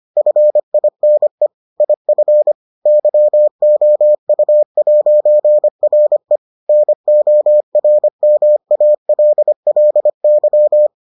Morse code, heard as a keyed tone, about 25 wpm, 600 Hz. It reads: FINE IF YOU'RE NORMALLY